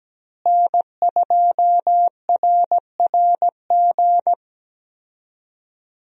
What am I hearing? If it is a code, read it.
Morse code, heard as a keyed tone, about 17 wpm, 700 Hz. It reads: N2RRG
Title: USA Call Signs - 5 Characters - Repeated 3x - 17wpm